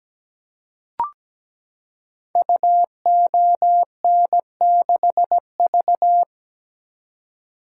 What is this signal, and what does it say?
Morse code, heard as a keyed tone, about 17 wpm, 700 Hz. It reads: UON6V
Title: Code Groups - 5 Letter:Numbers - 17wpm